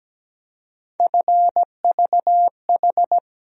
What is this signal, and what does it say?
Morse code, heard as a keyed tone, about 17 wpm, 700 Hz. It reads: FVH
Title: USA Call Sign Postfixes - 3 Letters 17wpm